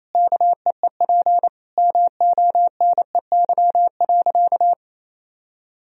Morse code, heard as a keyed tone, about 28 wpm, 700 Hz. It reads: KEEP MONEY.
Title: Sentences from Top 1000 Words - Review Part 4 - 28wpm